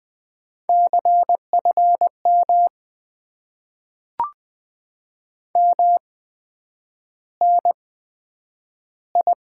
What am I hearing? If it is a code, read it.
Morse code, heard as a keyed tone, about 20 wpm, spaced as for 9 wpm, 700 Hz. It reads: CFM MNI
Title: Top 90 QSO Elements - 8x Character Spacing - 20wpm